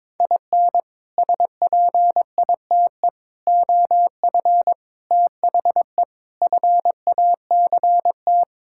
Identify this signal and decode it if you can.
Morse code, heard as a keyed tone, about 21 wpm, 700 Hz. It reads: IN SPITE OF THE FACT
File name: practice / 22wpm / Sets of 6 Words - 22wpm